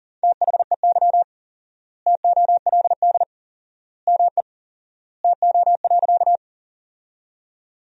Morse code, heard as a keyed tone, about 40 wpm, 700 Hz. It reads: THEY TOLD ME TO.
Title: 4x Word-Spacing - Sentences from Top 400 Words - 40wpm